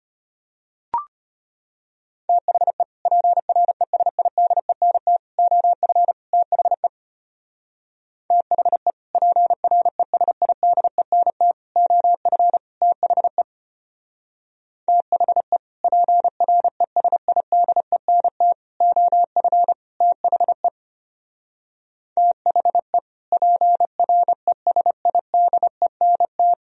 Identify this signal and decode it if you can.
Morse code, heard as a keyed tone, about 32 wpm, 700 Hz. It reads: THE PRESIDENT OF THE THE PRESIDENT OF THE THE PRESIDENT OF THE THE PRESIDENT